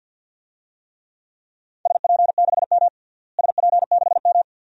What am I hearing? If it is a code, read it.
Morse code, heard as a keyed tone, about 49 wpm, 700 Hz. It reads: SP6M SP6M
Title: DXCC-100 Call Signs - 4 Characters - Repeated 2x - 50wpm